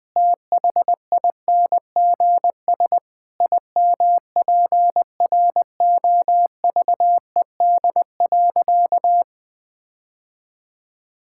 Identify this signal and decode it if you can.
Morse code, heard as a keyed tone, about 21 wpm, 700 Hz. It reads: THINGS IMPROVED.
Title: Sentences with 4-Word Phrases - 20wpm